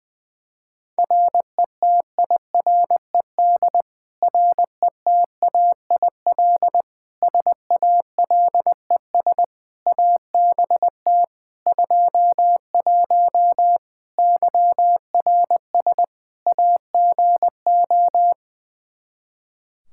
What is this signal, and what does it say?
Morse code, heard as a keyed tone, about 20 wpm, 700 Hz. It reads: RETIRED RETAIL SALES ABT 21 YRS AGO
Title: Practice QSO - Exchange 3 - More Info - 20wpm